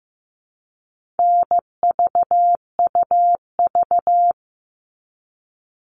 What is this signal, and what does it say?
Morse code, heard as a keyed tone, about 15 wpm, 700 Hz. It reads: NVUV
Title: All Cut Numbers - 4-Digit 15wpm